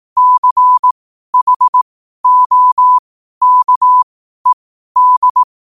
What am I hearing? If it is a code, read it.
Morse code, heard as a keyed tone, about 18 wpm, spaced as for 12 wpm, 1000 Hz. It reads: CHOKED
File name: qotdfiles / 12wpm